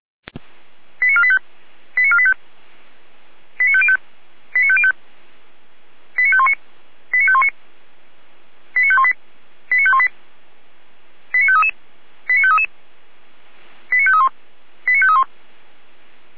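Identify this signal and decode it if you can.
Five-tone selective call (ZVEI), heard as quick runs of five tones, tones of about 70 ms. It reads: 89485 89485 89505 89505 89519 89519 89518 89518 89533 89533 89531 89531
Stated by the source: ZVEI five-tone selective call received by SDR, from the feuerwehr-orlamuende.org fire brigade server.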